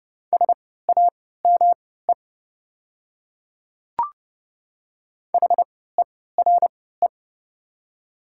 Morse code, heard as a keyed tone, about 29 wpm, spaced as for 17 wpm, 690 Hz. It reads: SAME HERE
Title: 3x Character Spacing - Top 200 Words - 30wpm